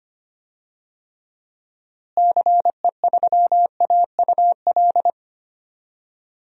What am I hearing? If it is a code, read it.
Morse code, heard as a keyed tone, about 25 wpm, 700 Hz. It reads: CE3AUL
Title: DXCC-100 Call Signs - 6 Characters - Repeated 2x - 25wpm